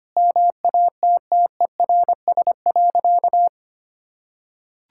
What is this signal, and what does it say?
Morse code, heard as a keyed tone, about 25 wpm, 700 Hz. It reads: MATTERS.
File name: practice / 25wpm / Sentences with 4-Word Phrases - 25wpm